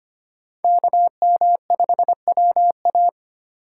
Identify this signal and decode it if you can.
Morse code, heard as a keyed tone, about 25 wpm, 700 Hz. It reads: KM5WA
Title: USA Call Signs - 5 Characters - 2x Character Spacing - 25wpm